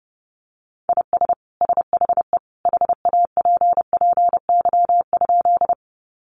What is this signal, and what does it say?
Morse code, heard as a keyed tone, about 30 wpm, 700 Hz. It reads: IS SHE HAPPY?
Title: Sentences from Top 1000 Words 30wpm